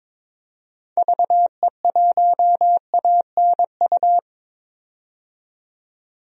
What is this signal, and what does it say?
Morse code, heard as a keyed tone, about 22 wpm, 700 Hz. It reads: VE1ANU
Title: DXCC-100 Call Signs - 6 Characters - 22wpm